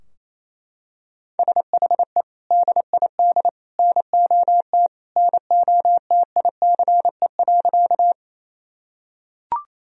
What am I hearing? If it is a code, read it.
Morse code, heard as a keyed tone, about 28 wpm, 700 Hz. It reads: SHE DID NOT NOTICE.
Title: Sentences from Top 1000 Words - Review Part 2 - 28wpm